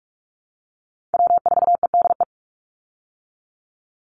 Morse code, heard as a keyed tone, about 44 wpm, 700 Hz. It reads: W4EDE
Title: Call Signs Repeated 2x - 45wpm